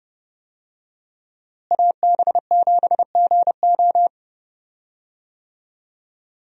Morse code, heard as a keyed tone, about 30 wpm, 700 Hz. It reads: AB7GO
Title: USA Call Signs - 5 Characters - Repeated 2x - 30wpm